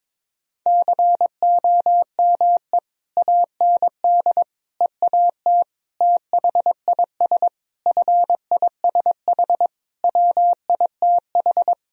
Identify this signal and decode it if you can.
Morse code, heard as a keyed tone, about 22 wpm, 700 Hz. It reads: COME AND EAT THIS FISH WITH
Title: Sentences from Top 1000 Words - Review Part 1 - 22wpm